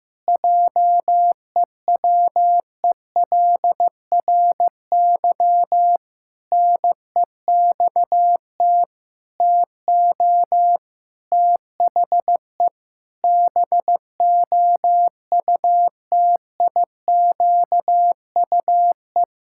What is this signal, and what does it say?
Morse code, heard as a keyed tone, about 15 wpm, 700 Hz. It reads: JEWELRY NEXT TO THE BOUTIQUE
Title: Pangram Challenge v2 - 15wpm